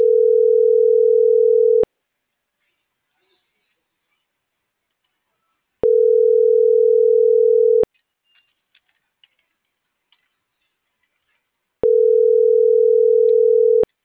Ringback tone (North America), 440 + 480 Hz.